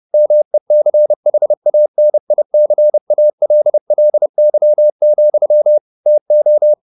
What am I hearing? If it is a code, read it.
Morse code, heard as a keyed tone, about 30 wpm, 600 Hz. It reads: MECHANICALLY, TO
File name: the old man and the sea / the old man and the sea 1 30wpm